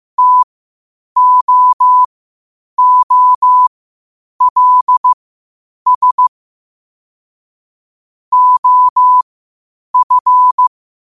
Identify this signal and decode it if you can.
Morse code, heard as a keyed tone, about 15 wpm, spaced as for 8 wpm, 1000 Hz. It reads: TOOLS OF
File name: 8wpm